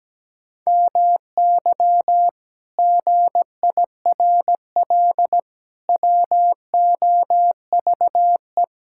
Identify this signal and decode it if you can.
Morse code, heard as a keyed tone, about 17 wpm, 700 Hz. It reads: MY GIRL WOVE
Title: Pangram Challenge v2 - 17wpm